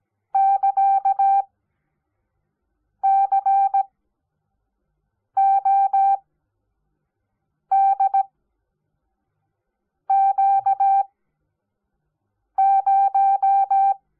Morse code, heard as a keyed tone, about 17 wpm, spaced as for 5 wpm, 780 Hz. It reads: <KA> CODQ0
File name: Onlinelehrgang / morsen / DJ4UF Morsen05